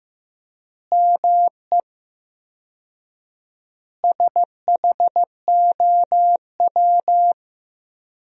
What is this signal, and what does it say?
Morse code, heard as a keyed tone, about 15 wpm, 700 Hz. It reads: ME SHOW
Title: 4x Word-Spacing - Sentences from Top 500 Words - 15wpm